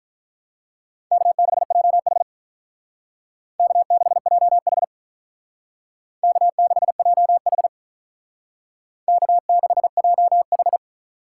Morse code, heard as a keyed tone, about 46 wpm, 700 Hz. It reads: K6JH K6JH K6JH K6JH